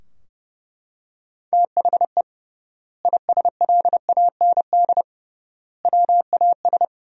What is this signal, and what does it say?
Morse code, heard as a keyed tone, about 30 wpm, 700 Hz. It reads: THE ISLAND WAS
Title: 3x Word-Spacing - Sentences from Top 500 Words - 30wpm